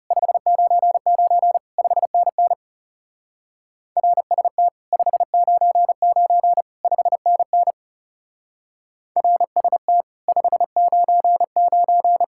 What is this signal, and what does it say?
Morse code, heard as a keyed tone, about 35 wpm, 700 Hz. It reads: 599 5NN RST 599 5NN RST 599
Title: Speed-Racing Common QSO Phrases - 30wpm